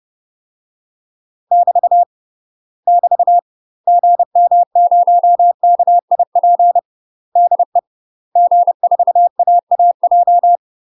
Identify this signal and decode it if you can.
Morse code, heard as a keyed tone, about 30 wpm, 700 Hz. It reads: <BT> <BT> GM0KIP DE G4AAJ
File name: UK MASTER QSOs TRAINING / 30WPM / UK MASTER QSOs 0052